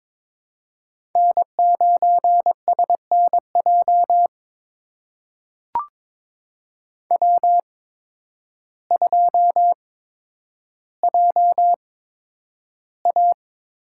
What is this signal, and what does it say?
Morse code, heard as a keyed tone, about 23 wpm, spaced as for 10 wpm, 700 Hz. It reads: N9SNJ W2JA
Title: USA Call Signs - 8x Character Spacing - 22wpm